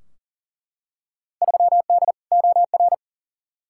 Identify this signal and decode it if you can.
Morse code, heard as a keyed tone, about 40 wpm, 700 Hz. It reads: 3D OR
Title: Field Day Exchanges - 40wpm